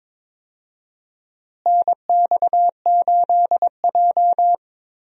Morse code, heard as a keyed tone, about 22 wpm, 700 Hz. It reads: NX8J
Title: USA Call Signs - 4 Characters - Repeated 3x - 22wpm